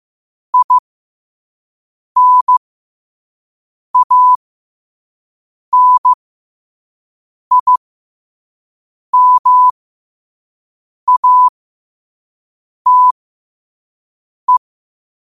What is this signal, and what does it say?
Morse code, heard as a keyed tone, about 15 wpm, spaced as for 5 wpm, 1000 Hz. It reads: INANIMATE
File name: qotdfiles / 5wpm